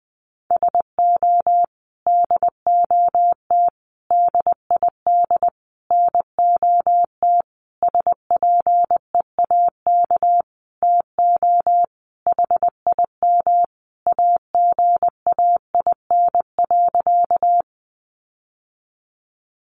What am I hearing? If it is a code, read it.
Morse code, heard as a keyed tone, about 19 wpm, 700 Hz. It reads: SO DOT DID NOT SPEAK TO HIM AGAIN.
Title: Sentences from Top 1500 Words 20wpm